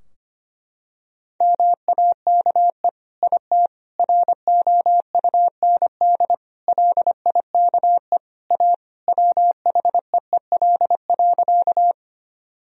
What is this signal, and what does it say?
Morse code, heard as a keyed tone, about 25 wpm, 700 Hz. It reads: MAKE IT ROUND LIKE A WHEEL.
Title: Sentences from Top 1000 Words - Review Part 3 - 25wpm